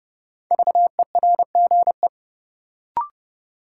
Morse code, heard as a keyed tone, about 32 wpm, 700 Hz. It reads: VERGE
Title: Sets of 4 Words - New Word Drill - 30wpm